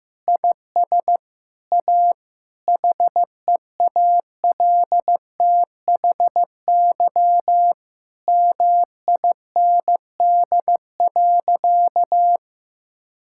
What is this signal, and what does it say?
Morse code, heard as a keyed tone, about 16 wpm, 700 Hz. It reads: IS A HEALTHY MIND.
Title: Sentences from Top 1000 Words - Review Part 5 - 15wpm